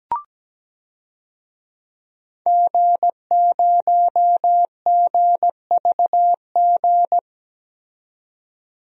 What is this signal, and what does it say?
Morse code, heard as a keyed tone, about 17 wpm, 700 Hz. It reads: G0GVG
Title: DXCC-100 Call Signs - 5 Characters - 17wpm